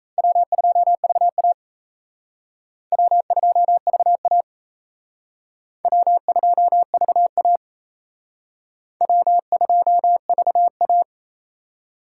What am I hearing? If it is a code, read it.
Morse code, heard as a keyed tone, about 37 wpm, 700 Hz. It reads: W2VA W2VA W2VA W2VA